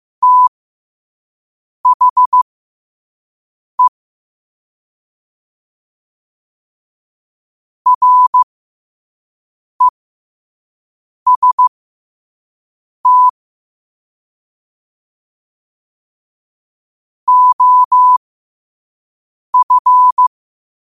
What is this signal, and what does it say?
Morse code, heard as a keyed tone, about 15 wpm, spaced as for 5 wpm, 1000 Hz. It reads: THE REST OF
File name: qotdfiles / 5wpm